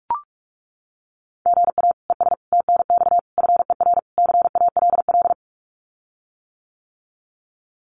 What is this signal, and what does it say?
Morse code, heard as a keyed tone, about 44 wpm, 700 Hz. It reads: GA ES TNX FER CALL
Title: Common QSO Phrases Encore 45wpm